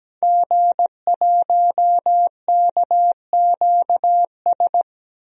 Morse code, heard as a keyed tone, about 17 wpm, 700 Hz. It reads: G1KQS
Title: DXCC-100 Call Signs - 5 Characters - Repeated 2x - 17wpm